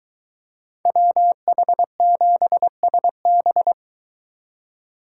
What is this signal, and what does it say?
Morse code, heard as a keyed tone, about 22 wpm, 700 Hz. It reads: WH7SB